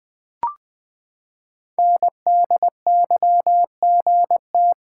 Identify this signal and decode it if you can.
Morse code, heard as a keyed tone, about 20 wpm, 700 Hz. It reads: NDYGT
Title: Code Groups - 5 Letter:Numbers - 20wpm